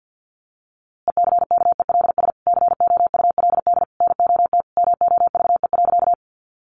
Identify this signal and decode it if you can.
Morse code, heard as a keyed tone, about 50 wpm, 700 Hz. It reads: ECKELS COULD NOT MOVE.
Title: A Sound Of Thunder 50wpm